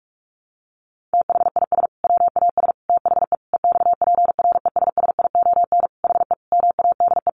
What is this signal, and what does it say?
Morse code, heard as a keyed tone, about 44 wpm, 700 Hz. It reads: THIS WAS THE EXPRESSION HE MADE
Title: Sentences from Top 1200 Words 45wpm